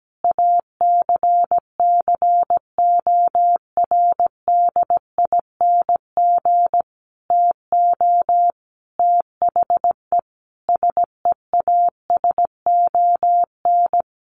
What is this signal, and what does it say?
Morse code, heard as a keyed tone, about 17 wpm, 700 Hz. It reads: ACCORDING TO THE SEASON
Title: Sentences from Top 1400 Words 17wpm